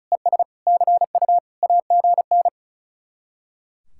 Morse code, heard as a keyed tone, about 45 wpm, 700 Hz. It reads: ES CU AGN